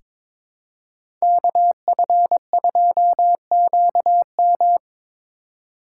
Morse code, heard as a keyed tone, about 22 wpm, 700 Hz. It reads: KF2QM
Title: USA Call Signs - 3x Character Spacing - 22wpm